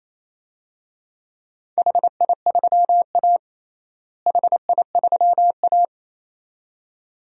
Morse code, heard as a keyed tone, about 28 wpm, 700 Hz. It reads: HI3A HI3A
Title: DXCC-100 Call Signs - 4 Characters - Repeated 2x - 28wpm